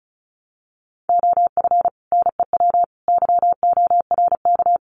Morse code, heard as a keyed tone, about 34 wpm, 700 Hz. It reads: OF NEW YORK
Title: Sets of 3 Words 35wpm